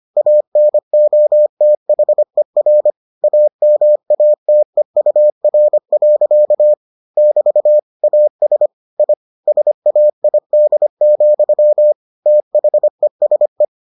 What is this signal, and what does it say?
Morse code, heard as a keyed tone, about 25 wpm, 600 Hz. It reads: ANOTHER AMATEUR. <BT> AS I SAID, THESE